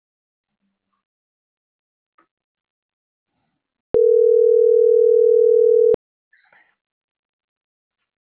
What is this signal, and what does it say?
Ringback tone (North America), 440 + 480 Hz.
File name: CENTRAL NAVIERAS / out onelinecapacitacion - .30363